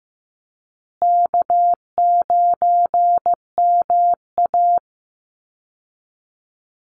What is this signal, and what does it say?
Morse code, heard as a keyed tone, about 15 wpm, 700 Hz. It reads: K9MA
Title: Easy Call Signs Repeated 3x - 15wpm